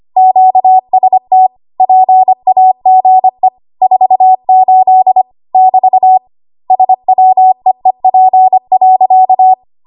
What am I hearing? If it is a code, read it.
Morse code, heard as a keyed tone, about 25 wpm, 750 Hz. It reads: QST PAGE 48 <BT> SWEEP.
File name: Morse / 25WPM